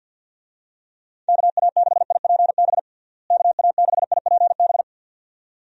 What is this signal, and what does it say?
Morse code, heard as a keyed tone, about 51 wpm, 700 Hz. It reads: KA6IPB KA6IPB
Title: USA Call Signs - 6 Characters - Repeated 2x - 50wpm